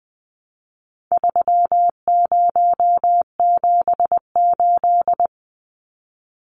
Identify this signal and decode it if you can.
Morse code, heard as a keyed tone, about 20 wpm, 700 Hz. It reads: 3078
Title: Numbers - 4-Digit 20wpm